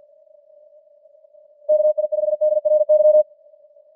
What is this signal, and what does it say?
Morse code, heard as a keyed tone, about 50 wpm, 610 Hz. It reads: KI5DRY